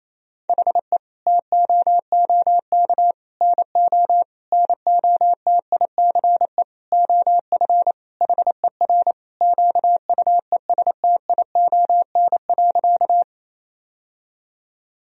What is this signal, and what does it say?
Morse code, heard as a keyed tone, about 28 wpm, 700 Hz. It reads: HE TOOK NO NOTICE OF HER QUESTION.
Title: Sentences from Top 1000 Words - Review Part 2 - 28wpm